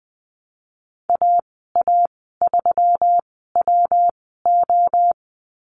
Morse code, heard as a keyed tone, about 20 wpm, spaced as for 15 wpm, 700 Hz. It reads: AA3WO
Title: 186 Lesson 30g - Letter b - Call Signs 20wpm